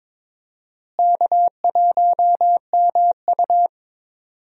Morse code, heard as a keyed tone, about 22 wpm, 700 Hz. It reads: K1MU
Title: USA Call Signs - 4 Characters - Repeated 3x - 22wpm